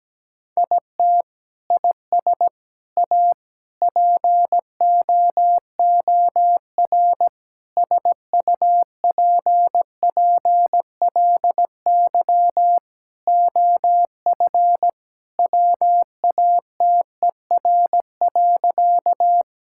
Morse code, heard as a keyed tone, about 17 wpm, 700 Hz. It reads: IT IS A POOR SUPPLY OF WATER.
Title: Sentences from Top 1000 Words - Review Part 5 - 17wpm